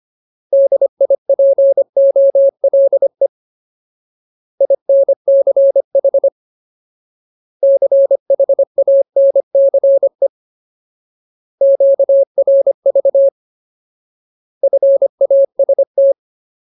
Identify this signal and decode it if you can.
Morse code, heard as a keyed tone, about 25 wpm, 550 Hz. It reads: DIPOLE INCH CHANCE QRV FAST
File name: Ultimate Word List / 25WPM Ultimate